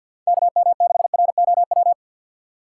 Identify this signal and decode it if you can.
Morse code, heard as a keyed tone, about 48 wpm, 700 Hz. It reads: KM6ROW
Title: USA Call Signs - 6 Characters - 4x Character Spacing - 50wpm